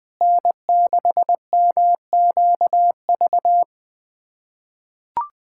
Morse code, heard as a keyed tone, about 20 wpm, 700 Hz. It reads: N6MQV
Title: USA Callsign DE Callsign 20wpm